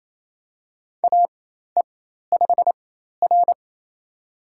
Morse code, heard as a keyed tone, about 28 wpm, spaced as for 13 wpm, 700 Hz. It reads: AE5R
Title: USA Call Signs - 4 Characters - 4x Character Spacing - 28wpm